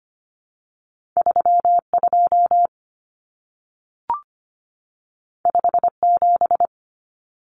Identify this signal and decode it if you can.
Morse code, heard as a keyed tone, about 25 wpm, 700 Hz. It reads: 3 2 5 7
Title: Common Cut Numbers - 2-Digit 25wpm